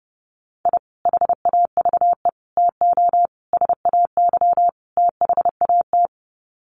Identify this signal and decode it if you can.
Morse code, heard as a keyed tone, about 30 wpm, 700 Hz. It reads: I HAVE TO SAY THAT
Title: Sets of 5 Words Encore 30wpm